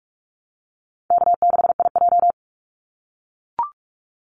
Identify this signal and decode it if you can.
Morse code, heard as a keyed tone, about 44 wpm, 700 Hz. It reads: K6IJ
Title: Easy Call Signs - 45wpm